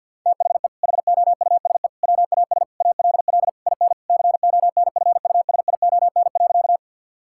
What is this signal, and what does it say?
Morse code, heard as a keyed tone, about 50 wpm, 700 Hz. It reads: THE HOUSE WAS ALL IN CONFUSION.
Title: Sentences from Top 1600 Words 50wpm